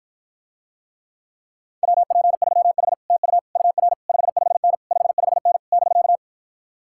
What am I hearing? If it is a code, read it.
Morse code, heard as a keyed tone, about 52 wpm, 700 Hz. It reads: WP3H TU UR 55N 55N <BK>